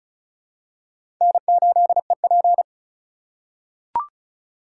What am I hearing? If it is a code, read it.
Morse code, heard as a keyed tone, about 34 wpm, 700 Hz. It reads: N8EP
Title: USA Call Signs - 2x Character Spacing - 35wpm